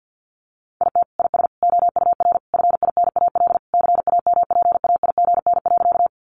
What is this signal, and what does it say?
Morse code, heard as a keyed tone, about 50 wpm, 700 Hz. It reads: IT IS OUR FINAL CAMPAIGN.
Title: Sentences from Top 1200 Words 50wpm